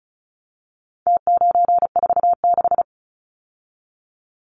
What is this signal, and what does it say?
Morse code, heard as a keyed tone, about 34 wpm, 700 Hz. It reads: T946
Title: Numbers - Contest Style - 4-Digit 35wpm